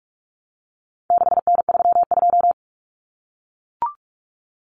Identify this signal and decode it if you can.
Morse code, heard as a keyed tone, about 44 wpm, 700 Hz. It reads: 6N32
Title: Common Cut Numbers - 4-Digit 45wpm